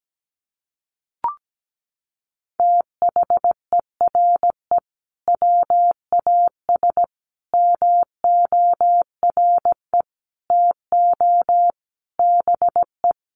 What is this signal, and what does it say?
Morse code, heard as a keyed tone, about 17 wpm, 700 Hz. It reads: THERE WAS MORE TO BE
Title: Sentences from Top 500 Words 17wpm